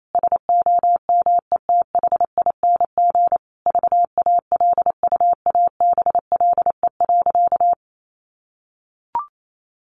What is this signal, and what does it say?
Morse code, heard as a keyed tone, about 28 wpm, 700 Hz. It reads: SOMETHING VALUABLE.
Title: Sentences from Top 1400 Words 28wpm